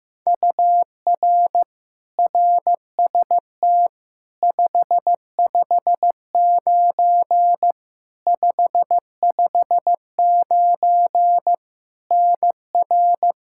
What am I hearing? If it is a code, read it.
Morse code, heard as a keyed tone, about 15 wpm, 700 Hz. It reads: UR RST 559 559 NR
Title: Practice QSO - Exchange 1 - Beginning - 15wpm